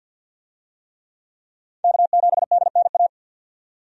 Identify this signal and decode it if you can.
Morse code, heard as a keyed tone, about 48 wpm, 700 Hz. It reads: K7DNA
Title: USA Call Signs - 5 Characters - 50wpm